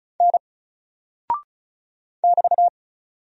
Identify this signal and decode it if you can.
Morse code, heard as a keyed tone, about 34 wpm, 700 Hz. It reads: N <BT>
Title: Single Character (Letter:Number:Punctuation:Prosign) 35wpm